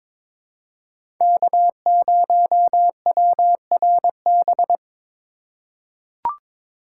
Morse code, heard as a keyed tone, about 22 wpm, 700 Hz. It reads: K0WRB
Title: USA Call Signs - 5 Characters - 3x Character Spacing - 22wpm